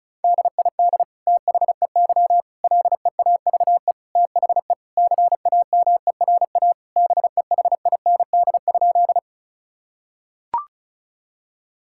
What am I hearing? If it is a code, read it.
Morse code, heard as a keyed tone, about 36 wpm, 700 Hz. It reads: DID THEY LEAVE THE CAMERA BEHIND?
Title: Sentences from Top 1000 Words - Review Part 5 - 35wpm